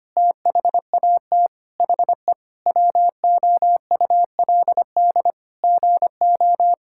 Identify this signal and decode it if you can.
Morse code, heard as a keyed tone, about 25 wpm, 700 Hz. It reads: THAT HE WOULD GO
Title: Sentences from Top 1600 Words 25wpm